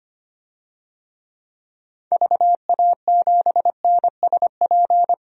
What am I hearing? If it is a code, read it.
Morse code, heard as a keyed tone, about 25 wpm, 700 Hz. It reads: VA7NSP
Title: DXCC-100 Call Signs - 6 Characters - 25wpm